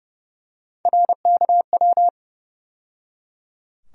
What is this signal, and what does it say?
Morse code, heard as a keyed tone, about 31 wpm, 700 Hz. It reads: RKW
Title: USA Call Sign Postfixes - 3 Letters 30wpm